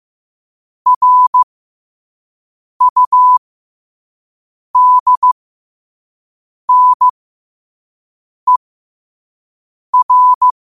Morse code, heard as a keyed tone, about 15 wpm, spaced as for 5 wpm, 1000 Hz. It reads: RUDNER
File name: qotdfiles / 5wpm